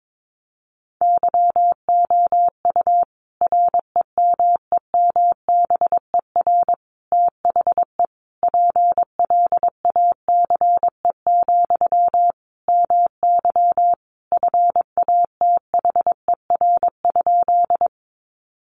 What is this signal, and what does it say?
Morse code, heard as a keyed tone, about 22 wpm, 700 Hz. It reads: YOU REMEMBER THE PLACE, MY FATHER?
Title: Sentences from Top 400 Words 22wpm